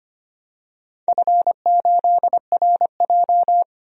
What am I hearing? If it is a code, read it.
Morse code, heard as a keyed tone, about 25 wpm, 700 Hz. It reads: F8RJ
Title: DXCC-100 Call Signs - 4 Characters - 25wpm